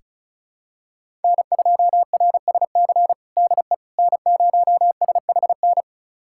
Morse code, heard as a keyed tone, about 34 wpm, 700 Hz. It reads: N2RSC DE N0SHN
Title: USA Callsign DE Callsign - 5 Characters - 35wpm